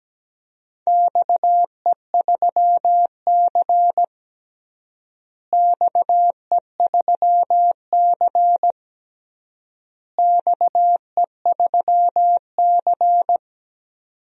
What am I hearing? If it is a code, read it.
Morse code, heard as a keyed tone, about 17 wpm, 700 Hz. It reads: XE3C XE3C XE3C
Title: DXCC-100 Call Signs - 4 Characters - Repeated 3x - 17wpm